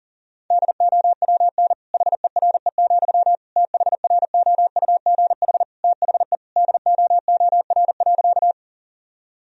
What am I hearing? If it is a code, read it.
Morse code, heard as a keyed tone, about 40 wpm, 700 Hz. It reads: DOWN HERE, THROUGH THE DOOR.
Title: Sentences from Top 1000 Words - Review Part 1 - 40wpm